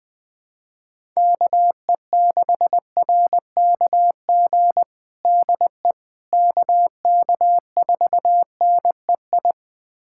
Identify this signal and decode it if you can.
Morse code, heard as a keyed tone, about 20 wpm, 700 Hz. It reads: KE6RKG DE KK4NEI
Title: USA Callsign DE Callsign 20wpm